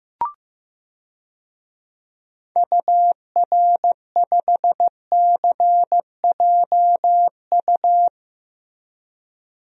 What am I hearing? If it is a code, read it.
Morse code, heard as a keyed tone, about 16 wpm, 700 Hz. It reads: UR5CJU
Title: DXCC-100 Call Signs - 6 Characters - Repeated 2x - 15wpm